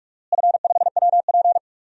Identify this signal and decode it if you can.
Morse code, heard as a keyed tone, about 46 wpm, 700 Hz. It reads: F5WP
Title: DXCC-100 Call Signs - 4 Characters - Repeated 3x - 45wpm